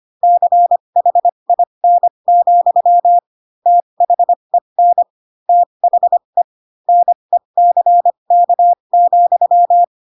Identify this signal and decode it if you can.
Morse code, heard as a keyed tone, about 25 wpm, spaced as for 22 wpm, 700 Hz. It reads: CHIN, THEN THE NECK,
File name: Pinocchio / Pinocchio ch3 18 25 ik1hge